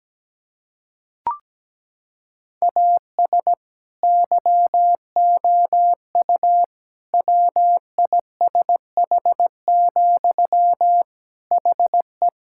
Morse code, heard as a keyed tone, about 17 wpm, 700 Hz. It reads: AS YOU WISH, HE
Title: Sentences from Top 1600 Words 17wpm